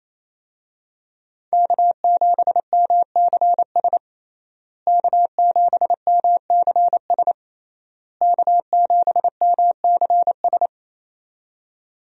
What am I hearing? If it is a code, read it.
Morse code, heard as a keyed tone, about 27 wpm, 700 Hz. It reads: K7MCS K7MCS K7MCS
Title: USA Call Signs - 5 Characters - Repeated 3x - 28wpm